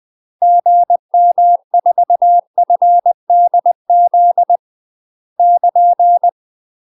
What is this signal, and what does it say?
Morse code, heard as a keyed tone, about 20 wpm, 700 Hz. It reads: GM4FDZ <KN>
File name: UK MASTER QSOs TRAINING / 20WPM / UK MASTER QSOs 0043